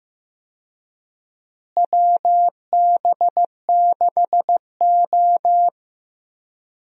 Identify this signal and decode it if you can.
Morse code, heard as a keyed tone, about 15 wpm, 700 Hz. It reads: WB6O
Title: USA Call Signs - 4 Characters - Repeated 2x - 15wpm